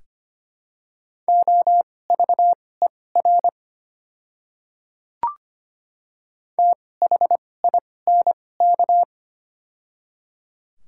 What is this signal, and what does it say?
Morse code, heard as a keyed tone, about 25 wpm, spaced as for 18 wpm, 700 Hz. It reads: OVER THINK
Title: 2x Character Spacing - Top 300 Words - 25wpm